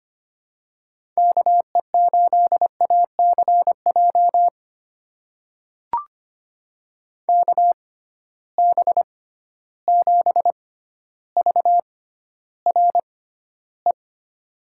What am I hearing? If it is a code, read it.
Morse code, heard as a keyed tone, about 25 wpm, spaced as for 13 wpm, 700 Hz. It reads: KE8ACJ KB7VRE
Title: USA Call Signs - 6 Characters - 6x Character Spacing - 25wpm